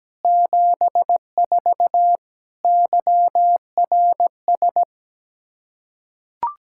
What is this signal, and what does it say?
Morse code, heard as a keyed tone, about 17 wpm, 700 Hz. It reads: 74 YRS
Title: Practice QSO - Exchange 3 - More Info - 17wpm